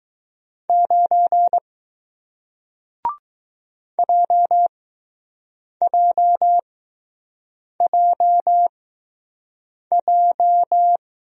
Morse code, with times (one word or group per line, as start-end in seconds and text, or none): 0.7-1.6 9
4.0-11.0 JJJJ